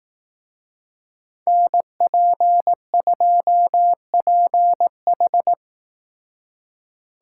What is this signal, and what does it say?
Morse code, heard as a keyed tone, about 18 wpm, 700 Hz. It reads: NP2PH
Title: Speed Racing - USA 2x2 Call Signs 15wpm